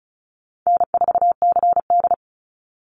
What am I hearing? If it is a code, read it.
Morse code, heard as a keyed tone, about 34 wpm, 700 Hz. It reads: N4CD
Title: Easy Call Signs Repeated 3x - 35wpm